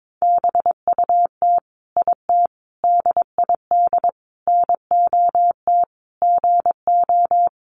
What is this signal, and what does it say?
Morse code, heard as a keyed tone, about 22 wpm, 700 Hz. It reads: BUT IT DID NOT GO
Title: Sentences from Top 600 Words 22wpm